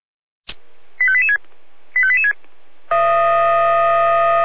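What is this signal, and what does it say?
Five-tone selective call (ZVEI), heard as quick runs of five tones, tones of about 70 ms. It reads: 85886 85886
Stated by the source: ZVEI five-tone selective call received on SDR.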